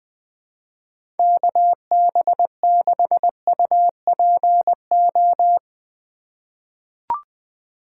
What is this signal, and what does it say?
Morse code, heard as a keyed tone, about 20 wpm, 700 Hz. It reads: KB6UPO
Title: USA Call Signs - 6 Characters - 3x Character Spacing - 20wpm